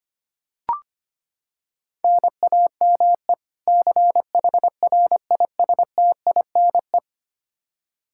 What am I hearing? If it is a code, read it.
Morse code, heard as a keyed tone, about 25 wpm, 700 Hz. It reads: NAME CHRISTINE
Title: Practice QSO - Exchange 1 - Beginning - 25wpm